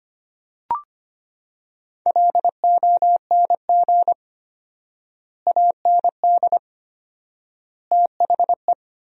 Morse code, heard as a keyed tone, about 25 wpm, 700 Hz. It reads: LONG AND THE
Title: 4x Word-Spacing - Sentences from Top 400 Words - 25wpm